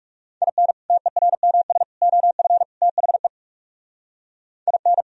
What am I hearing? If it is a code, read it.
Morse code, heard as a keyed tone, about 37 wpm, 700 Hz. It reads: IN TERMS OF THE IN
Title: Speed-Racing - Sets of 4 Words 30wpm